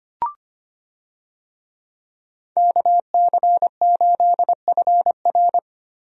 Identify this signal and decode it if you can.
Morse code, heard as a keyed tone, about 25 wpm, 700 Hz. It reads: KC8FR
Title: USA Call Signs - 5 Characters - Repeated 2x - 25wpm